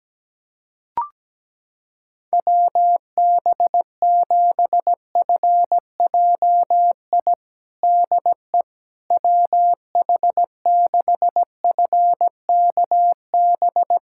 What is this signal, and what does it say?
Morse code, heard as a keyed tone, about 17 wpm, 700 Hz. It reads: WB7FJI DE WH6FKB
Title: USA Callsign DE Callsign - 6 Characters - 17wpm